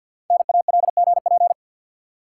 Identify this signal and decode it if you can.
Morse code, heard as a keyed tone, about 51 wpm, 700 Hz. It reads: NALGP
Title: Code Groups - 5 Letters - 50wpm